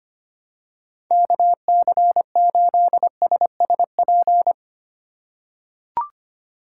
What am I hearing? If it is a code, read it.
Morse code, heard as a keyed tone, about 25 wpm, 700 Hz. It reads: KC8SSP
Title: USA Call Signs - 8x Character Spacing - 25wpm